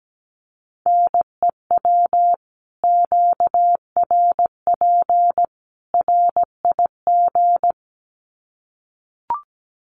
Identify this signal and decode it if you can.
Morse code, heard as a keyed tone, about 17 wpm, 700 Hz. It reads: NEW QRP RIG
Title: Common QSO Phrases 17wpm